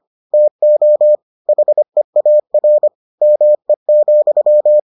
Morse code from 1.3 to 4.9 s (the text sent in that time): HEAR ME,